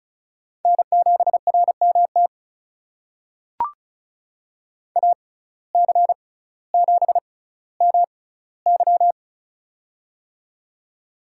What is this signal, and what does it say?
Morse code, heard as a keyed tone, about 36 wpm, spaced as for 18 wpm, 700 Hz. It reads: N7RMT AC7MY
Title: USA Call Signs - 5 Characters - 6x Character Spacing - 35wpm